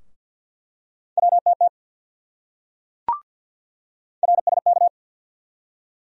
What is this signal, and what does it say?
Morse code, heard as a keyed tone, about 47 wpm, 700 Hz. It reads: WTT RSK
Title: USA Call Sign Postfixes - 3 Letters 50wpm